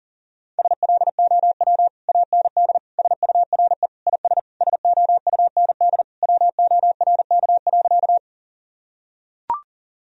Morse code, heard as a keyed tone, about 40 wpm, 700 Hz. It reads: SLOW AND SURE IS SOUND WORK.
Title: Sentences from Top 1000 Words - Review Part 2 - 40wpm